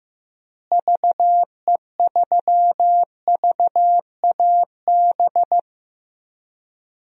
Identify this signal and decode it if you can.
Morse code, heard as a keyed tone, about 15 wpm, 700 Hz. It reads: VE3VAB
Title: DXCC-100 Call Signs - 6 Characters - Repeated 3x - 15wpm